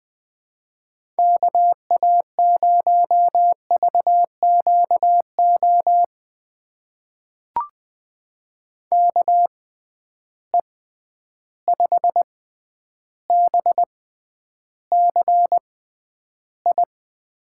Morse code, heard as a keyed tone, about 20 wpm, spaced as for 10 wpm, 700 Hz. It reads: KA0VQO KE5BCI
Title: USA Call Signs - 6 Characters - 6x Character Spacing - 20wpm